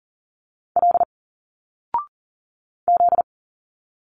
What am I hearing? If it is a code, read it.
Morse code, heard as a keyed tone, about 38 wpm, 700 Hz. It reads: L Z
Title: Single Letters 40wpm